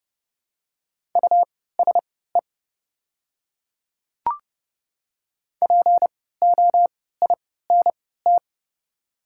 Morse code, heard as a keyed tone, about 30 wpm, spaced as for 17 wpm, 700 Hz. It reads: USE POINT
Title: 3x Character Spacing - Top 200 Words - 30wpm